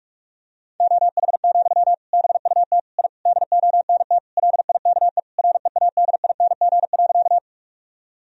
Morse code, heard as a keyed tone, about 46 wpm, 700 Hz. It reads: OH, BUT I DONT LIKE READING.
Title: Sentences from Top 1000 Words - Review Part 3 - 45wpm